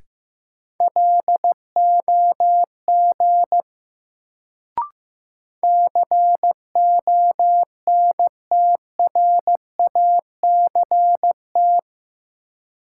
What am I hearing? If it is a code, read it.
Morse code, heard as a keyed tone, about 15 wpm, 700 Hz. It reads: LOG CONTRACT
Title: Sentences from Top 1400 Words - New Word Drill Alt - 15wpm